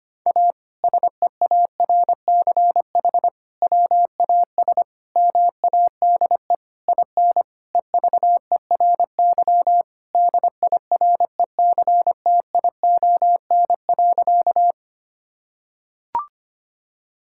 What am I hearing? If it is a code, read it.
Morse code, heard as a keyed tone, about 25 wpm, 700 Hz. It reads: A SEARCH WAS MADE IN EVERY DIRECTION.
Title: Sentences from Top 1000 Words - Review Part 5 - 25wpm